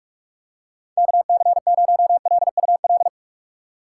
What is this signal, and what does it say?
Morse code, heard as a keyed tone, about 46 wpm, 700 Hz. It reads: KC0LUL
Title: USA Call Signs - 6 Characters - 8x Character Spacing - 45wpm